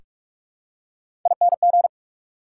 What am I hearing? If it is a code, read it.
Morse code, heard as a keyed tone, about 46 wpm, 690 Hz. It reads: ING
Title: Prefixes and Suffixes 45wpm